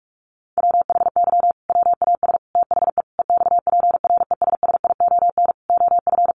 Morse code, heard as a keyed tone, about 44 wpm, 700 Hz. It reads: WHY WAS THE EXPRESSION OF
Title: Sentences from Top 1400 Words 45wpm